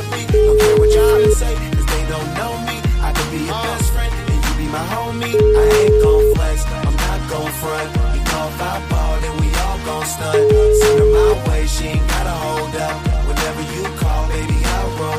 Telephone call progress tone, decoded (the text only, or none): ringback tone